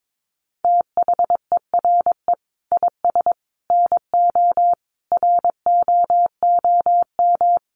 Morse code, heard as a keyed tone, about 22 wpm, 700 Hz. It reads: THERE IS NO ROOM
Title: Sentences from Top 1200 Words 22wpm